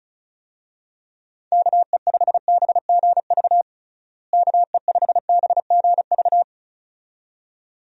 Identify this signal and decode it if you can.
Morse code, heard as a keyed tone, about 34 wpm, 700 Hz. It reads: KE5BGV KE5BGV
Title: USA Call Signs - 6 Characters - Repeated 2x - 35wpm